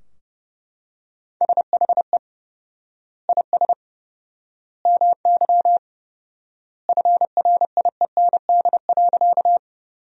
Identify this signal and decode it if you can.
Morse code, heard as a keyed tone, about 31 wpm, 700 Hz. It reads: SHE IS MY FRIEND.
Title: 4x Word-Spacing - Sentences from Top 300 Words - 30wpm